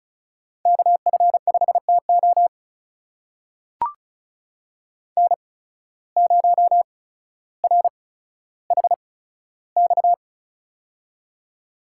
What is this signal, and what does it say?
Morse code, heard as a keyed tone, about 36 wpm, spaced as for 15 wpm, 700 Hz. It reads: KF5TO N0RHX
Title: USA Call Signs - 5 Characters - 8x Character Spacing - 35wpm